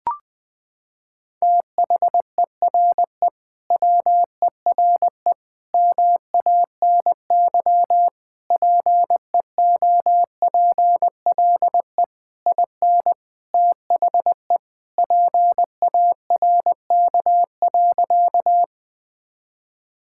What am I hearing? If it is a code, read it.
Morse code, heard as a keyed tone, about 20 wpm, 700 Hz. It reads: THERE WERE MANY PEOPLE IN THE PARK.
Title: Sentences with 3-Word Phrases - 20wpm